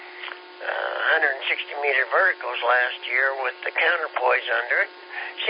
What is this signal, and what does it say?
Emergency Alert System (SAME) tones, heard as harsh buzzing bursts, with speech. It is heard from 1.0 to 4.9 s.